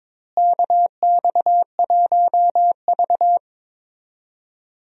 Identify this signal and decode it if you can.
Morse code, heard as a keyed tone, about 22 wpm, 700 Hz. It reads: KX1V
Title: USA Call Signs - 4 Characters - Repeated 3x - 22wpm